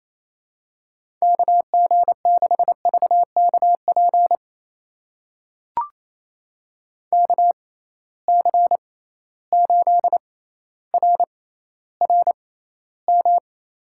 Morse code, heard as a keyed tone, about 28 wpm, spaced as for 14 wpm, 700 Hz. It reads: KG6VKP KC8RRM
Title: USA Call Signs - 6 Characters - 6x Character Spacing - 28wpm